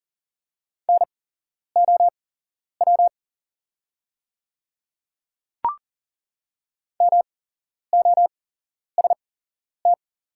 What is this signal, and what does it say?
Morse code, heard as a keyed tone, about 41 wpm, spaced as for 11 wpm, 700 Hz. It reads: NOW MOST
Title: 8x Character Spacing - Top 100 Words - 40wpm